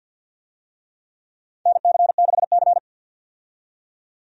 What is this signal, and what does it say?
Morse code, heard as a keyed tone, about 50 wpm, 700 Hz. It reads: NC6C
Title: USA Call Signs - 4 Characters - 50wpm